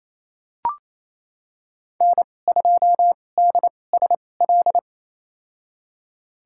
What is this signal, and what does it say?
Morse code, heard as a keyed tone, about 28 wpm, spaced as for 20 wpm, 700 Hz. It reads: N2DSL
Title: USA Call Signs - 5 Characters - 2x Character Spacing - 28wpm